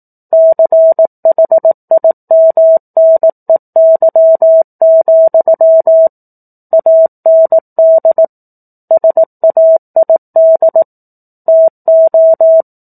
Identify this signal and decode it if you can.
Morse code, heard as a keyed tone, about 18 wpm, 650 Hz. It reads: CHIMNEY, AND SAID TO
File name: 04-Chapter-04 18 WPM